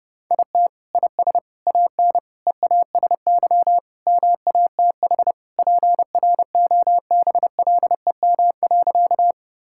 Morse code, heard as a keyed tone, about 30 wpm, 700 Hz. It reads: IT IS AN EASY MATH PROBLEM.
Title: Sentences from Top 1000 Words - Review Part 6 - 30wpm